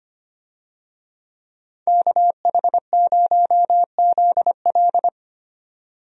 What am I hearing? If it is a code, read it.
Morse code, heard as a keyed tone, about 25 wpm, 700 Hz. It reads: KH0ZL
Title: USA Call Signs - 5 Characters - 25wpm